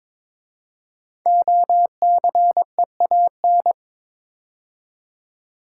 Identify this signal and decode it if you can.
Morse code, heard as a keyed tone, about 22 wpm, 700 Hz. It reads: OCEAN
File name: 5 Letter Words - First 3 Letters Given - 22wpm